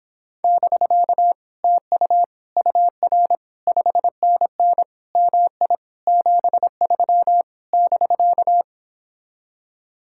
Morse code, heard as a keyed tone, about 26 wpm, 700 Hz. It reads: <BK> TU UR 5NN MI 73 <BK>